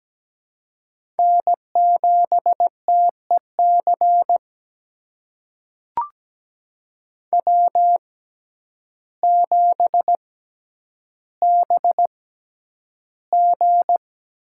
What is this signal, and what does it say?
Morse code, heard as a keyed tone, about 17 wpm, spaced as for 9 wpm, 700 Hz. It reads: N7TEC W7BG
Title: USA Call Signs - 6x Character Spacing - 17wpm